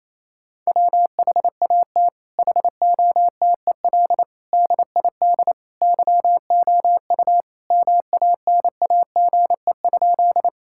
Morse code, heard as a keyed tone, about 28 wpm, 700 Hz. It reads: WHAT HOTEL DID YOU MANAGE?
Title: Sentences from Top 1600 Words 28wpm